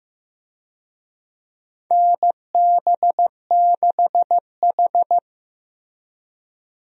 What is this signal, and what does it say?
Morse code, heard as a keyed tone, about 16 wpm, 700 Hz. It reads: NB6H
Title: USA Call Signs - 4 Characters - Repeated 3x - 15wpm